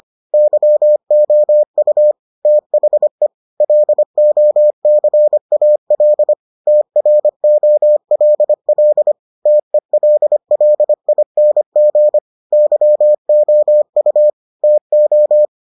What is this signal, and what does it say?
Morse code, heard as a keyed tone, about 25 wpm, 600 Hz. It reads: YOU THE LOCAL TROLL TELLING YOU TO